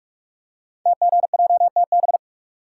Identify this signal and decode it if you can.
Morse code, heard as a keyed tone, about 44 wpm, 700 Hz. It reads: TGJTB
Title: Code Groups - 5 Letters - 45wpm